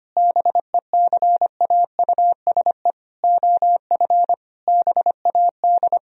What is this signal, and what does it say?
Morse code, heard as a keyed tone, about 25 wpm, 700 Hz. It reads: BECAUSE OF BAD
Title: Sentences from Top 1000 Words - Review Part 4 - 25wpm